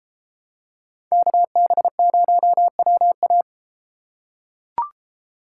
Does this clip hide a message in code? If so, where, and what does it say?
Yes: Morse code from 1.1 to 3.4 s, heard as a keyed tone, about 31 wpm, 700 Hz. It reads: KB0WA